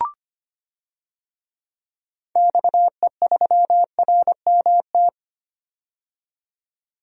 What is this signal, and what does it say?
Morse code, heard as a keyed tone, about 25 wpm, 700 Hz. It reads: XE3RMT
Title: DXCC-100 Call Signs - 6 Characters - 25wpm